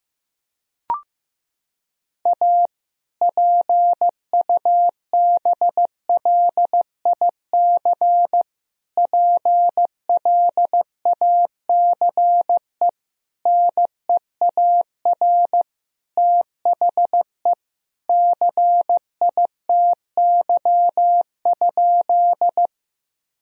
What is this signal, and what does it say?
Morse code, heard as a keyed tone, about 14 wpm, 700 Hz. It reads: A PUBLIC PLACE NEAR THE CITY?
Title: Sentences from Top 1000 Words - Review Part 3 - 15wpm